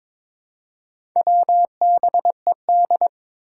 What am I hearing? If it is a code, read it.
Morse code, heard as a keyed tone, about 22 wpm, 700 Hz. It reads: WBED
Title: Four Challenging Letter-Number Combinations 22wpm